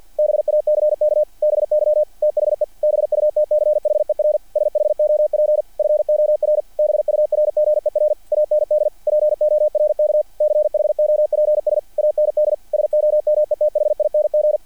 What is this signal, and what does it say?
Morse code, heard as a keyed tone, about 48 wpm, 600 Hz. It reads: BACK BY THE BUTCHER SHOP FOR BURGER AND PORK CHOPS AND SOMETHING
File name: archived / CRSnet